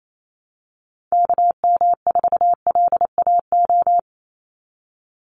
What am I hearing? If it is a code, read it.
Morse code, heard as a keyed tone, about 28 wpm, 700 Hz. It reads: KM4LAO
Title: Call Signs Repeated 3x - 28wpm